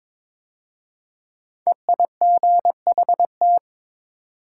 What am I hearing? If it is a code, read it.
Morse code, heard as a keyed tone, about 22 wpm, 700 Hz. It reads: EIGHT
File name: 5 Letter Words - First 3 Letters Given - 22wpm